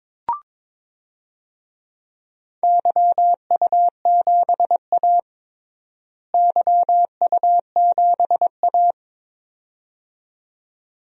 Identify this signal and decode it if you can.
Morse code, heard as a keyed tone, about 22 wpm, 700 Hz. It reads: YU7A YU7A
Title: DXCC-100 Call Signs - 4 Characters - Repeated 2x - 22wpm